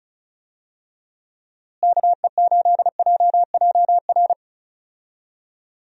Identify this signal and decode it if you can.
Morse code, heard as a keyed tone, about 36 wpm, 700 Hz. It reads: KE8JJR
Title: USA Call Signs - 6 Characters - Repeated 3x - 35wpm